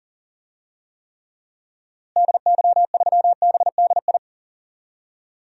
Morse code, heard as a keyed tone, about 40 wpm, 700 Hz. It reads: DY3BDI
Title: DXCC-100 Call Signs - 6 Characters - Repeated 3x - 40wpm